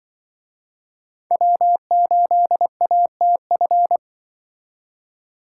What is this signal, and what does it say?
Morse code, heard as a keyed tone, about 24 wpm, 700 Hz. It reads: W8ATF
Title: Speed Racing - USA Call Signs 20wpm